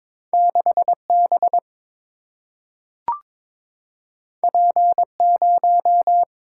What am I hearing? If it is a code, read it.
Morse code, heard as a keyed tone, about 22 wpm, 700 Hz. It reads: 6 B P 0
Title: Single-Letter Single-Number 22wpm